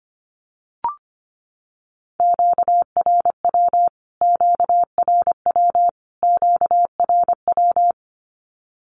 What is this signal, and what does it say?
Morse code, heard as a keyed tone, about 25 wpm, 700 Hz. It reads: QRW QRW QRW
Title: Q-Codes Repeated 3x 25wpm